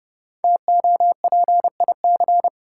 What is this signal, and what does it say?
Morse code, heard as a keyed tone, about 31 wpm, 700 Hz. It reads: TOPIC
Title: Sentences from Top 1200 Words - New Word Drill Alt - 30wpm